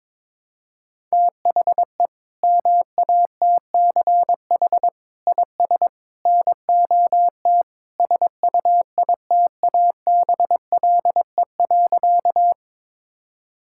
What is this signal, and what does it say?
Morse code, heard as a keyed tone, about 21 wpm, 700 Hz. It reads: THE MATCH IS NOT SUITABLE.
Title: Sentences from Top 1600 Words 22wpm